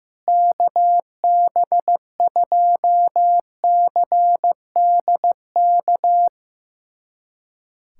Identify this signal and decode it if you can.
Morse code, heard as a keyed tone, about 14 wpm, 700 Hz. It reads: KB2CDK
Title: USA Callsign DE Callsign 15wpm